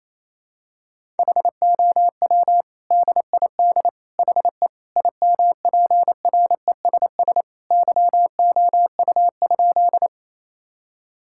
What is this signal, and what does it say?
Morse code, heard as a keyed tone, about 28 wpm, 700 Hz. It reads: HOW DID HE IMPRESS YOU?
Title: Sentences from Top 1600 Words 28wpm